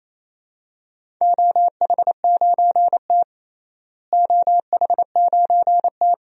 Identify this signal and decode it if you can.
Morse code, heard as a keyed tone, about 27 wpm, 700 Hz. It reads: OH9T OH9T
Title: DXCC-100 Call Signs - 4 Characters - Repeated 2x - 28wpm